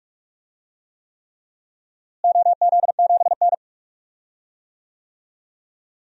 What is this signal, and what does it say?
Morse code, heard as a keyed tone, about 45 wpm, 700 Hz. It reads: OZ7N
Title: DXCC-100 Call Signs - 4 Characters - Repeated 3x - 45wpm